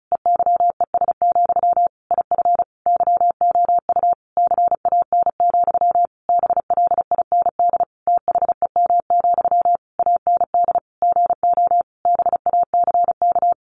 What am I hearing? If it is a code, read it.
Morse code, heard as a keyed tone, about 34 wpm, 700 Hz. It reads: EYES, IF YOU CAN, BLIND THEM, AND GO BACK
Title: A Sound Of Thunder 35wpm